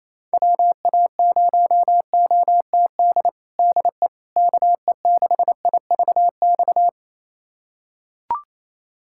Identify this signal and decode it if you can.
Morse code, heard as a keyed tone, about 28 wpm, 700 Hz. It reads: WA0OTD DE KE6IVX
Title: USA Callsign DE Callsign 28wpm